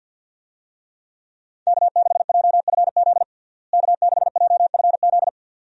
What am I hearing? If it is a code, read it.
Morse code, heard as a keyed tone, about 47 wpm, 700 Hz. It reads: K6JFZ K6JFZ
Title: USA Call Signs - 5 Characters - Repeated 2x - 50wpm